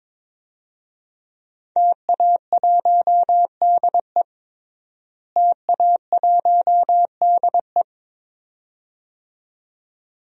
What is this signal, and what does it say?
Morse code, heard as a keyed tone, about 22 wpm, 700 Hz. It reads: TA1DE TA1DE
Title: DXCC-100 Call Signs - 5 Characters - Repeated 2x - 22wpm